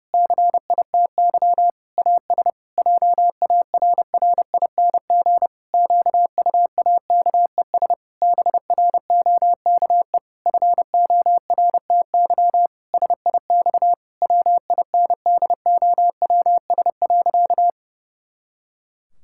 Morse code, heard as a keyed tone, about 30 wpm, 700 Hz. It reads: CITY AS JARRING QUAKES BROKE FORTY SIX WINDOWS.
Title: Pangram Challenge v2 - 30wpm